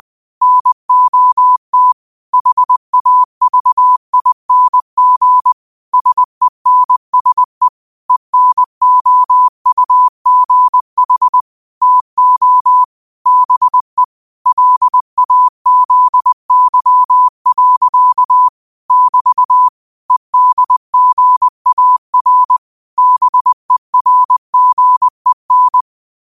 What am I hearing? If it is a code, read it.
Morse code, heard as a keyed tone, about 20 wpm, 1000 Hz. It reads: NOT HAVING SENSE ENOUGH TO BE LAZY. <BT> EDGAR BERGEN
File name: qotdfiles / 20wpm